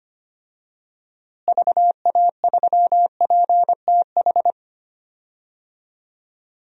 Morse code, heard as a keyed tone, about 25 wpm, 700 Hz. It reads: VA3PTH
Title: DXCC-100 Call Signs - 6 Characters - Repeated 2x - 25wpm